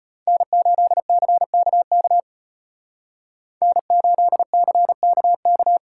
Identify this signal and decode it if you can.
Morse code, heard as a keyed tone, about 33 wpm, 700 Hz. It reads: N8CKK N8CKK